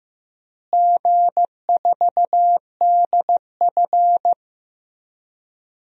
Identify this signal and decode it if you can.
Morse code, heard as a keyed tone, about 16 wpm, 700 Hz. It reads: G4DF
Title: DXCC-100 Call Signs - 4 Characters - Repeated 3x - 15wpm